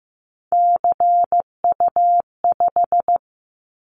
Morse code, heard as a keyed tone, about 15 wpm, 700 Hz. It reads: CU5
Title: Callsign Prefixes for Top 50 DXCC - 15wpm